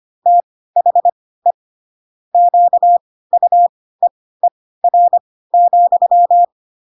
Morse code, heard as a keyed tone, about 25 wpm, spaced as for 16 wpm, 700 Hz. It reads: THE QUEER,
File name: Pinocchio / Pinocchio ch5 10 25 ik1hge